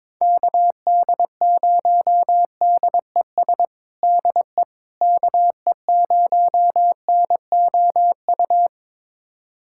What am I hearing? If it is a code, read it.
Morse code, heard as a keyed tone, about 22 wpm, 700 Hz. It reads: KD0DES DE KE0NOU
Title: USA Callsign DE Callsign - 6 Characters - 22wpm